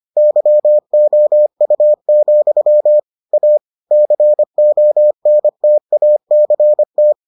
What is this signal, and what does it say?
Morse code, heard as a keyed tone, about 25 wpm, 600 Hz. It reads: YOU, A CONTACT